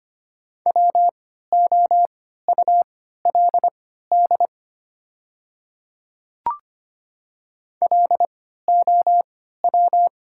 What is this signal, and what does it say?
Morse code, heard as a keyed tone, about 25 wpm, spaced as for 14 wpm, 700 Hz. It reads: WOULD LOW
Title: 3x Character Spacing - Top 500 Words - 25wpm